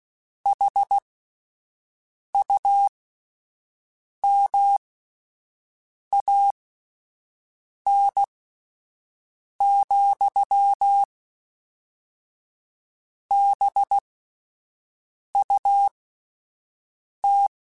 Morse code, heard as a keyed tone, about 16 wpm, spaced as for 5 wpm, 800 Hz. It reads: HUMAN,BUT
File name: Spaced / Chuckles Spaced-mp3s / HumanErr sp @10